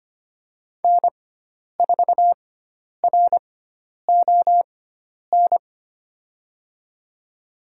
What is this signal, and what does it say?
Morse code, heard as a keyed tone, about 25 wpm, spaced as for 10 wpm, 700 Hz. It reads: N4RON
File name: Lesson 16g - Letter c - Call Signs 25wpm